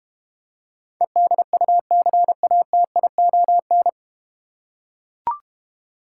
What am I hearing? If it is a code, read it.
Morse code, heard as a keyed tone, about 31 wpm, 690 Hz. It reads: EDUCATION
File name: Easy Words - Speed-Racing with Hints - 25wpm